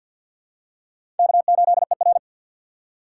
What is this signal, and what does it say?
Morse code, heard as a keyed tone, about 48 wpm, 700 Hz. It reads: K8ER
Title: USA Call Signs - 4 Characters - 6x Character Spacing - 50wpm